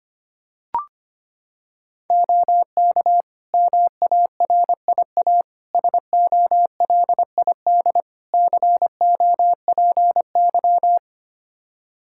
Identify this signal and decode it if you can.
Morse code, heard as a keyed tone, about 25 wpm, 700 Hz. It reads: OK MARIA SOLID COPY
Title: Practice QSO - Exchange 3 - More Info - 25wpm